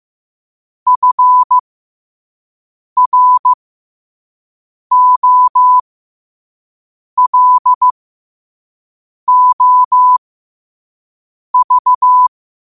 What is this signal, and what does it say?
Morse code, heard as a keyed tone, about 15 wpm, spaced as for 5 wpm, 1000 Hz. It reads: FROLOV
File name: qotdfiles / 5wpm